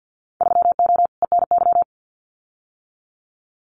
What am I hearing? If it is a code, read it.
Morse code, heard as a keyed tone, about 50 wpm, 700 Hz. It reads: 3O ENY
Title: Winter Field Day Exchanges Encore 50wpm